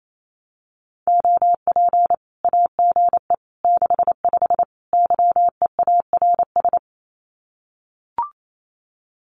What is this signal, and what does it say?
Morse code, heard as a keyed tone, about 28 wpm, 700 Hz. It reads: OP AGE 65 YEARS
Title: Common QSO Phrases 28wpm